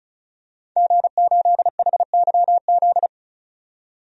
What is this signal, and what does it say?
Morse code, heard as a keyed tone, about 34 wpm, 700 Hz. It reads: G8HYZ
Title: DXCC-100 Call Signs - 5 Characters - 35wpm